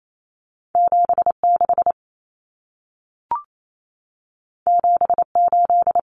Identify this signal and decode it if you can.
Morse code, heard as a keyed tone, about 28 wpm, 700 Hz. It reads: 76 78
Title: Numbers - 2-Digit 28wpm